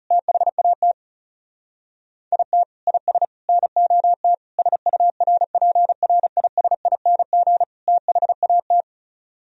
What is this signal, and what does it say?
Morse code, heard as a keyed tone, about 39 wpm, 700 Hz. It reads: THAT IT IS NOT SURPRISING THAT